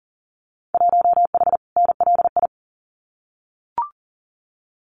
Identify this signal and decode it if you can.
Morse code, heard as a keyed tone, about 40 wpm, 700 Hz. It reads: 1H NLI
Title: Winter Field Day Exchanges 40wpm